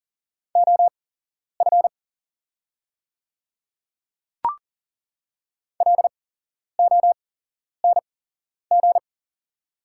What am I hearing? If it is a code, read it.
Morse code, heard as a keyed tone, about 40 wpm, spaced as for 11 wpm, 700 Hz. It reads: OF LONG
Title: 8x Character Spacing - Top 200 Words - 40wpm